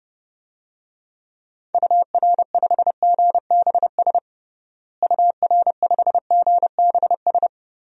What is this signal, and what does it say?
Morse code, heard as a keyed tone, about 30 wpm, 700 Hz. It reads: UR5GBS UR5GBS
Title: DXCC-100 Call Signs - 6 Characters - Repeated 2x - 30wpm